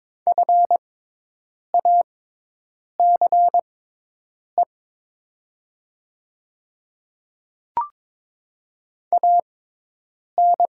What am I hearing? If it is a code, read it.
Morse code, heard as a keyed tone, about 21 wpm, spaced as for 8 wpm, 700 Hz. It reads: FACE AN
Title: 6x Character Spacing - Top 500 Words - 22wpm